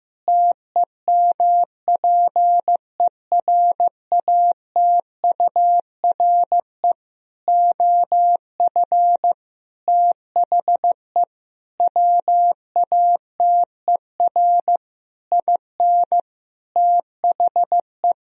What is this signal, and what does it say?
Morse code, heard as a keyed tone, about 15 wpm, 700 Hz. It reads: TEMPERATURE OF THE WATER IN THE
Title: Sentences from Top 1000 Words - Review Part 4 - 15wpm